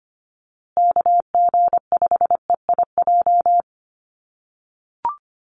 Morse code, heard as a keyed tone, about 25 wpm, 700 Hz. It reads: KG5EIJ
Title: USA Call Signs Repeated 3x 25wpm